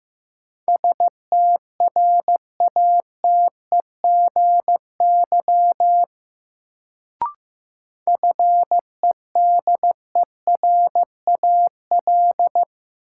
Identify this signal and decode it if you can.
Morse code, heard as a keyed tone, about 15 wpm, 700 Hz. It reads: STRATEGY FEDERAL
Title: Sentences from Top 1000 Words - Review Part 5 - New Word Drill - 15wpm